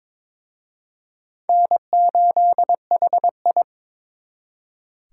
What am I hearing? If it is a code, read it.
Morse code, heard as a keyed tone, about 26 wpm, 700 Hz. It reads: N8HI
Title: Speed Racing - USA 1x2 Call Signs 22wpm